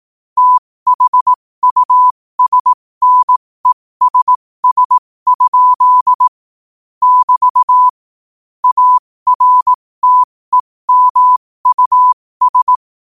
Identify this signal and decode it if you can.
Morse code, heard as a keyed tone, about 18 wpm, spaced as for 15 wpm, 1000 Hz. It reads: THUSNESS? <BT> ARTEMUS
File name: qotdfiles / 15wpm